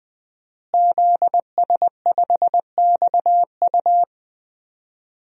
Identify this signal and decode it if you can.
Morse code, heard as a keyed tone, about 21 wpm, 700 Hz. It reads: ZS5XU
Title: DXCC-100 Call Signs - 5 Characters - Repeated 3x - 20wpm